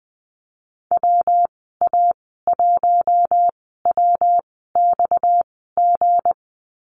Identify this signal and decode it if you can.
Morse code, heard as a keyed tone, about 20 wpm, spaced as for 14 wpm, 700 Hz. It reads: WA1WXG
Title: 238 Lesson 38g - Letter q - Call Signs 20wpm